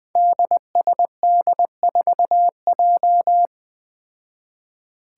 Morse code, heard as a keyed tone, about 20 wpm, 700 Hz. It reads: DSD4J
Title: Five Challenging Letter-Number Combinations 20wpm